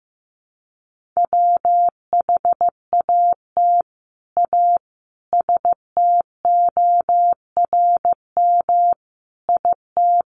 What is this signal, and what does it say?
Morse code, heard as a keyed tone, about 15 wpm, 700 Hz. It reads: WHAT A STORM IT
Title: Sentences from Top 1500 Words 15wpm